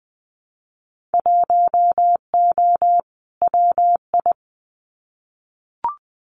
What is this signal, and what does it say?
Morse code, heard as a keyed tone, about 20 wpm, 700 Hz. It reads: 1O WI
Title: Winter Field Day Exchanges 20wpm